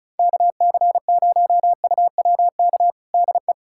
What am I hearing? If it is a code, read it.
Morse code, heard as a keyed tone, about 36 wpm, 700 Hz. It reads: KC0UWK DE
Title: USA Callsign DE Callsign - 6 Characters - 35wpm